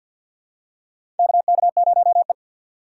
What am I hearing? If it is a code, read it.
Morse code, heard as a keyed tone, about 48 wpm, 700 Hz. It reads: KK0E
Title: USA Call Signs - 4 Characters - 3x Character Spacing - 50wpm